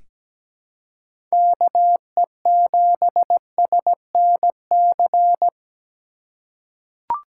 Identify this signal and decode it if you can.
Morse code, heard as a keyed tone, about 17 wpm, 700 Hz. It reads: KE7SNC
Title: USA Call Signs - 6 Characters - 2x Character Spacing - 17wpm